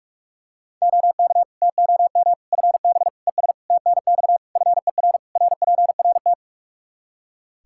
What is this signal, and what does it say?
Morse code, heard as a keyed tone, about 46 wpm, 700 Hz. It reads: OK TOM FB ES TNX FER RPRT
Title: Practice QSO - Exchange 2 - Info - 45wpm